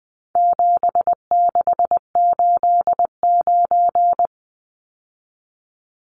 Morse code, heard as a keyed tone, about 20 wpm, 700 Hz. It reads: 7689
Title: Numbers - 4-Digit 20wpm